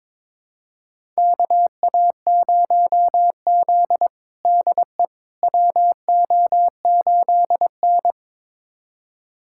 Morse code, heard as a keyed tone, about 22 wpm, 700 Hz. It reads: KA0Z DE WO8N
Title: USA Callsign DE Callsign - 4 Characters - 22wpm